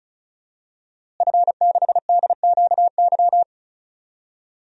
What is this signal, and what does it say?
Morse code, heard as a keyed tone, about 36 wpm, 700 Hz. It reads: F6DQY
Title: DXCC-100 Call Signs - 5 Characters - Repeated 3x - 35wpm